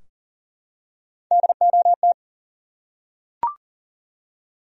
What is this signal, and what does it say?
Morse code, heard as a keyed tone, about 40 wpm, 700 Hz. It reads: DOT
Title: USA Common Names - Max 3 Letters - 40wpm